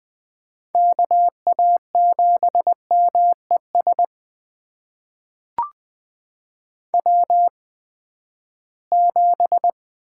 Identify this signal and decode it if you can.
Morse code, heard as a keyed tone, about 21 wpm, spaced as for 14 wpm, 700 Hz. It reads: KA7MES W7
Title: USA Call Signs - 8x Character Spacing - 20wpm